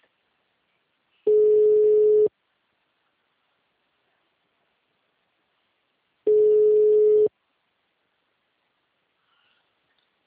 Ringback tone (Europe, 425 Hz).